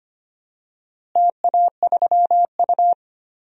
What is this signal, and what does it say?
Morse code, heard as a keyed tone, about 25 wpm, 700 Hz. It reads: TA3U
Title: DXCC-100 Call Signs - 4 Characters - Repeated 2x - 25wpm